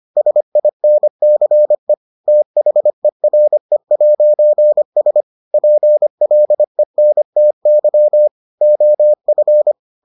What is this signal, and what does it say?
Morse code, heard as a keyed tone, about 25 wpm, 600 Hz. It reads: SINCE THERE'S PLENTY OF